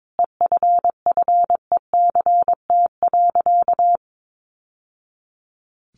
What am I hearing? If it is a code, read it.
Morse code, heard as a keyed tone, about 22 wpm, 700 Hz. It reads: EFFECT.
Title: Sentences from Top 1000 Words 22wpm